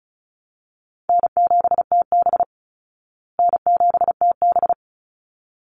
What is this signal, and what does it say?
Morse code, heard as a keyed tone, about 34 wpm, 700 Hz. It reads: N7TB N7TB
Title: Call Signs Repeated 3x - 35wpm